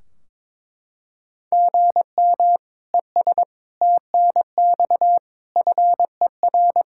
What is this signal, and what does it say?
Morse code, heard as a keyed tone, about 22 wpm, 700 Hz. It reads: GM ES TNX FER
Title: K1USN SST QSO Practice 22wpm